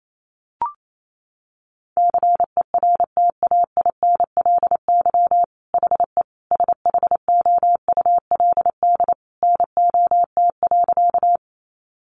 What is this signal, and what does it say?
Morse code, heard as a keyed tone, about 28 wpm, 690 Hz. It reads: CERTAINLY HE SHOULD NOT.
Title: Sentences from Top 900 Words 28wpm